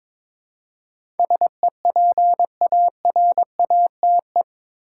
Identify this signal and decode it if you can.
Morse code, heard as a keyed tone, about 22 wpm, 700 Hz. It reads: SEPARATE
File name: practice / 22wpm / Eight Letter Words - 22wpm